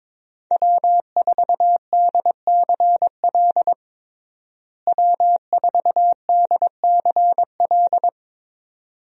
Morse code, heard as a keyed tone, about 22 wpm, 700 Hz. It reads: W4DCL W4DCL
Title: USA Call Signs - 5 Characters - Repeated 2x - 22wpm